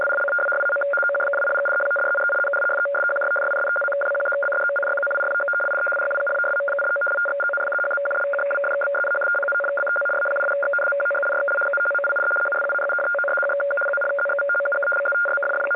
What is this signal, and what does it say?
Two-tone teleprinter signal, about 850 Hz shift.